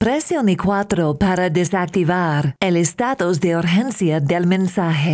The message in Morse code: none